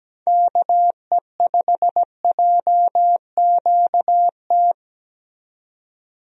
Morse code, heard as a keyed tone, about 17 wpm, 700 Hz. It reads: KE5JQT
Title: USA Call Signs - 6 Characters - 17wpm